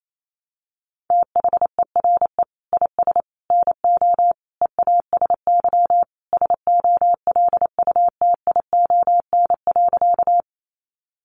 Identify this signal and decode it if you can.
Morse code, heard as a keyed tone, about 28 wpm, 700 Hz. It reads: THERE IS NO EASY SOLUTION.
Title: Sentences from Top 1300 Words 28wpm